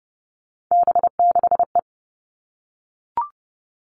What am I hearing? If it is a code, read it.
Morse code, heard as a keyed tone, about 30 wpm, 700 Hz. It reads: B6E
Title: All Cut Numbers - 3-Digit 30wpm